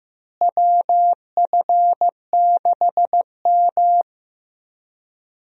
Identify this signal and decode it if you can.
Morse code, heard as a keyed tone, about 15 wpm, 700 Hz. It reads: WF6M
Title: USA Call Signs - 4 Characters - Repeated 3x - 15wpm